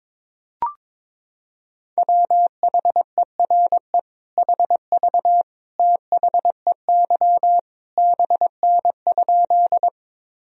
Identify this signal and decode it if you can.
Morse code, heard as a keyed tone, about 21 wpm, 700 Hz. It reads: WHERE HV THEY BN?
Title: Sentences from Top 200 Words and Abbreviations 22wpm